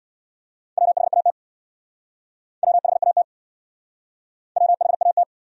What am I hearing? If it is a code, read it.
Morse code, heard as a keyed tone, about 65 wpm, 700 Hz. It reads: WHAT WHAT WHAT